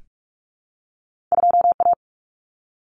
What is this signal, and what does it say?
Morse code, heard as a keyed tone, about 46 wpm, 700 Hz. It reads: 2A
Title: Common Cut Numbers - 2-Digit 45wpm